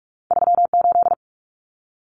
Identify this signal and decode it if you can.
Morse code, heard as a keyed tone, about 46 wpm, 700 Hz. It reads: 38
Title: Numbers - 2-Digit 45wpm